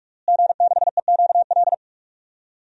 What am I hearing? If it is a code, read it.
Morse code, heard as a keyed tone, about 46 wpm, 700 Hz. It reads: G6EQL
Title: DXCC-100 Call Signs - 5 Characters - Repeated 3x - 45wpm